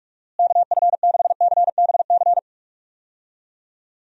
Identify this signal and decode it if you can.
Morse code, heard as a keyed tone, about 46 wpm, 700 Hz. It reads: KF6CBC
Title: USA Call Signs - 6 Characters - Repeated 2x - 45wpm